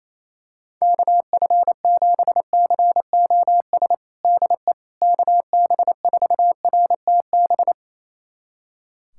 Morse code, heard as a keyed tone, about 28 wpm, 700 Hz. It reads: KF7COS DE KB4RTB
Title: USA Callsign DE Callsign - 6 Characters - 28wpm